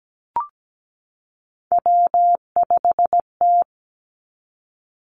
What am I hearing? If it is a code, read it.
Morse code, heard as a keyed tone, about 17 wpm, 700 Hz. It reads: W5T
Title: Three Challenging Letter-Number Combinations 17wpm